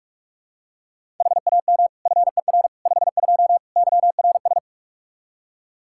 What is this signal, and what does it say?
Morse code, heard as a keyed tone, about 48 wpm, 700 Hz. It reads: HAM FER 52 YRS
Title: Practice QSO - Exchange 3 - More Info - 45wpm